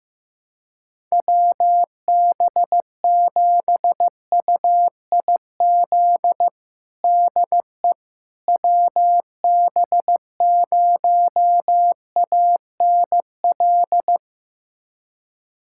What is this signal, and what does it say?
Morse code, heard as a keyed tone, about 14 wpm, 700 Hz. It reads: WB7UIZ DE WB0ANL
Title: USA Callsign DE Callsign - 6 Characters - 15wpm